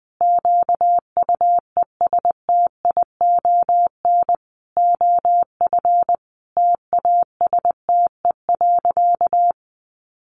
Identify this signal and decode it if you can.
Morse code, heard as a keyed tone, about 20 wpm, 700 Hz. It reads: QUESTION OF TASTE.
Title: Sentences from Top 1100 Words 20wpm